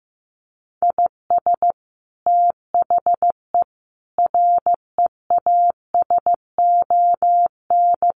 Morse code, heard as a keyed tone, about 14 wpm, 700 Hz. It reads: IS THE REASON
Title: Sentences from Top 700 Words 15wpm